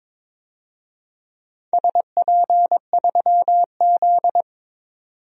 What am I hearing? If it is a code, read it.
Morse code, heard as a keyed tone, about 22 wpm, 700 Hz. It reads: SP3Z
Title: DXCC-100 Call Signs - 4 Characters - 22wpm